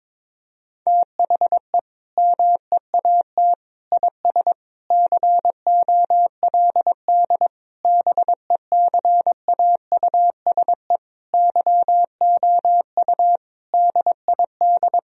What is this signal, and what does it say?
Morse code, heard as a keyed tone, about 21 wpm, 700 Hz. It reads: THE MEAT IS COLD BECAUSE YOU DID
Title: Sentences from Top 1000 Words - Review Part 3 - 22wpm